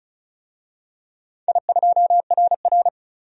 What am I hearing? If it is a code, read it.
Morse code, heard as a keyed tone, about 35 wpm, 700 Hz. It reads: I2RR
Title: DXCC-100 Call Signs - 4 Characters - 35wpm